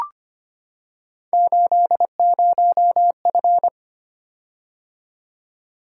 Morse code, heard as a keyed tone, about 25 wpm, 700 Hz. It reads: 80F
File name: Three Letter-Number Combinations - 25wpm